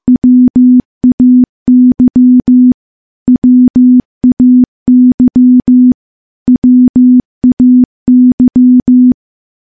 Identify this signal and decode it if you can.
Morse code, heard as a keyed tone, about 15 wpm, 250 Hz. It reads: WAY WAY WAY
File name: morse-training-15WPM